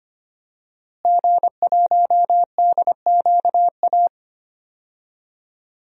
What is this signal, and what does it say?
Morse code, heard as a keyed tone, about 25 wpm, 700 Hz. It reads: G1DQA
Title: DXCC-100 Call Signs - 5 Characters - Repeated 2x - 25wpm